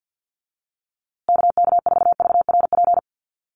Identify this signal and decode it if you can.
Morse code, heard as a keyed tone, about 48 wpm, 700 Hz. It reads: KK4VRP
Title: USA Call Signs Repeated 3x 50wpm